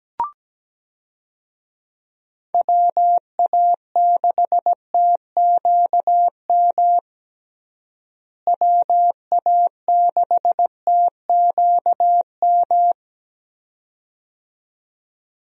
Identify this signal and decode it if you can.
Morse code, heard as a keyed tone, about 17 wpm, 700 Hz. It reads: WA6TQM WA6TQM
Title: USA Call Signs - 6 Characters - Repeated 2x - 17wpm